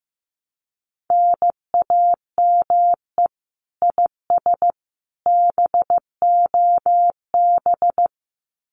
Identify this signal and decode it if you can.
Morse code, heard as a keyed tone, about 15 wpm, 700 Hz. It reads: NAME IS BOB
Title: Common QSO Phrases 15wpm